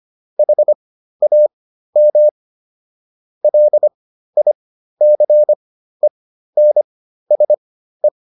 Morse code, heard as a keyed tone, about 25 wpm, spaced as for 13 wpm, 600 Hz. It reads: HAM LICENSE